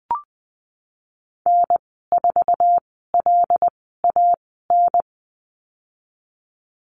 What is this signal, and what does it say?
Morse code, heard as a keyed tone, about 20 wpm, spaced as for 15 wpm, 700 Hz. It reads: N4LAN
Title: 088 Lesson 15g - Number 5 - Call Signs 20wpm